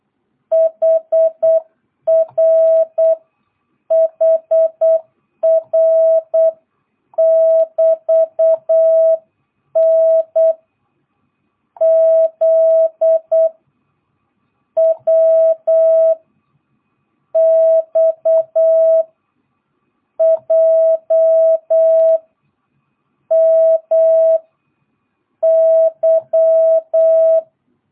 Morse code, heard as a keyed tone, about 8 wpm, 660 Hz. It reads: HR HR <BT> N Z W X J M Y
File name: jh8cbh / 2022cw